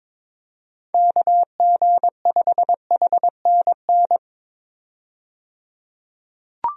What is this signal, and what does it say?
Morse code, heard as a keyed tone, about 22 wpm, 700 Hz. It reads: KG5HNN
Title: USA Call Signs - 6 Characters - 22wpm